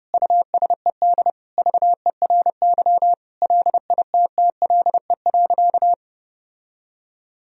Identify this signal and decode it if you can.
Morse code, heard as a keyed tone, about 30 wpm, 700 Hz. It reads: USED VERY LITTLE.
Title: Sentences from Top 1000 Words - Review Part 3 - 30wpm